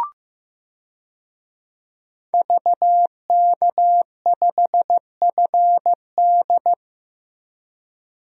Morse code, heard as a keyed tone, about 15 wpm, 700 Hz. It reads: VK5FD
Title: DXCC-100 Call Signs - 5 Characters - Repeated 3x - 15wpm